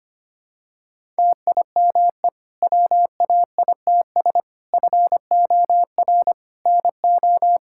Morse code, heard as a keyed tone, about 25 wpm, 700 Hz. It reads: TIME WAITS FOR NO
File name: practice / 25wpm / Difficult Sentences with Hints - 25wpm